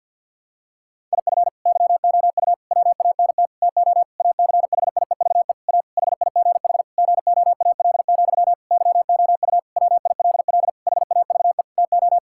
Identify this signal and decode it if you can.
Morse code, heard as a keyed tone, about 48 wpm, 700 Hz. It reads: IF YOU WANT TO ACHIEVE A HIGH GOAL, YOU WILL HAVE TO
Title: Sentences from Top 1000 Words - Review Part 6 - 50wpm